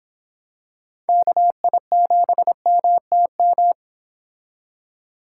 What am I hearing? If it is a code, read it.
Morse code, heard as a keyed tone, about 26 wpm, 700 Hz. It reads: KI7MTM